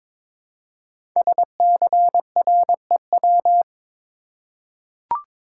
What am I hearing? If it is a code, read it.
Morse code, heard as a keyed tone, about 22 wpm, 700 Hz. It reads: SCREW
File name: 5 Letter Words - First 3 Letters Given - 22wpm